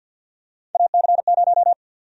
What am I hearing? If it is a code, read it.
Morse code, heard as a keyed tone, about 52 wpm, 700 Hz. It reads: AC0
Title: USA Call Sign Prefixes - Repeated 2x - 50wpm